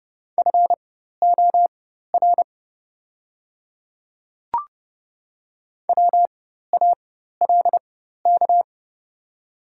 Morse code, heard as a keyed tone, about 31 wpm, spaced as for 14 wpm, 700 Hz. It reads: FOR WALK
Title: 4x Character Spacing - Top 500 Words - 30wpm